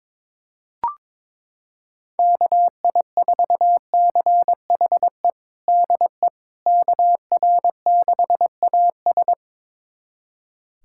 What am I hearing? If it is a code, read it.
Morse code, heard as a keyed tone, about 21 wpm, 700 Hz. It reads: KI4CHE DE KR6AS
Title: USA Callsign DE Callsign 22wpm